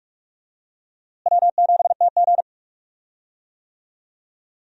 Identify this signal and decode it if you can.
Morse code, heard as a keyed tone, about 42 wpm, 700 Hz. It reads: W7TG
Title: USA Call Signs - 4 Characters - Repeated 2x - 45wpm